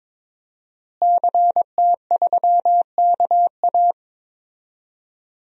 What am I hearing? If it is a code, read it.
Morse code, heard as a keyed tone, about 22 wpm, 700 Hz. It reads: CT3KA
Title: DXCC-100 Call Signs - 5 Characters - Repeated 3x - 22wpm